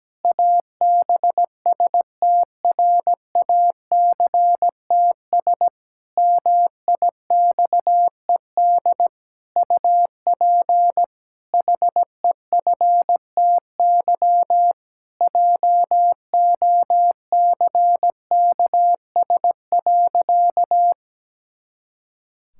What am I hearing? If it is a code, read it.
Morse code, heard as a keyed tone, about 17 wpm, 700 Hz. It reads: ABSTRACTS MIXED UP HEFTY JOCKS.
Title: Pangram Challenge v2 - 17wpm